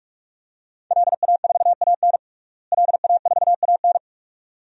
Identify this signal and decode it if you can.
Morse code, heard as a keyed tone, about 46 wpm, 700 Hz. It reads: LA4AN LA4AN
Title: DXCC-100 Call Signs - 5 Characters - Repeated 2x - 45wpm